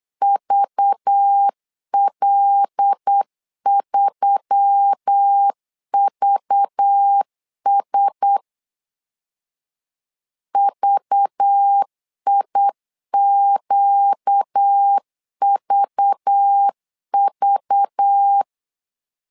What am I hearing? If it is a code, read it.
Morse code, heard as a keyed tone, about 8 wpm, 800 Hz. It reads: VL3VS VIQVV
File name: LEZIONI / 5 LBVC / 5 LBV 8wpm 2837